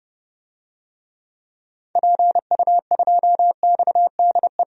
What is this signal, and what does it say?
Morse code, heard as a keyed tone, about 30 wpm, 700 Hz. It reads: PU2XDE
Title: DXCC-100 Call Signs - 6 Characters - 30wpm